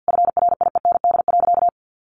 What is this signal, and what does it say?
Morse code, heard as a keyed tone, about 49 wpm, 700 Hz. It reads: FRIEND.
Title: Sentences from Top 600 Words 50wpm